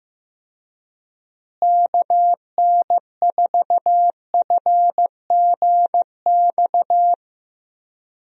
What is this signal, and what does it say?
Morse code, heard as a keyed tone, about 15 wpm, 700 Hz. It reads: KN4FGX
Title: USA Call Signs - 6 Characters - Repeated 2x - 15wpm